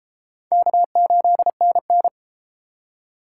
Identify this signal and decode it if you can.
Morse code, heard as a keyed tone, about 31 wpm, 700 Hz. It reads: K8NN